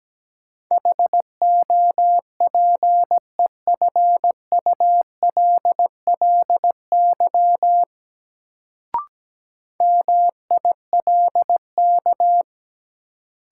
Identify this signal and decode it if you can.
Morse code, heard as a keyed tone, about 17 wpm, 700 Hz. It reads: HOPEFULLY MILK
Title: Sentences from Top 1500 Words - New Word Drill - 17wpm